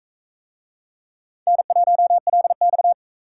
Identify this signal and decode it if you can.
Morse code, heard as a keyed tone, about 41 wpm, 700 Hz. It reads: N1LX